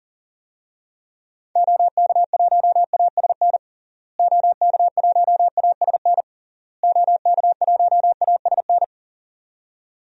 Morse code, heard as a keyed tone, about 37 wpm, 700 Hz. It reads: OK1ASN OK1ASN OK1ASN
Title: DXCC-100 Call Signs - 6 Characters - Repeated 3x - 40wpm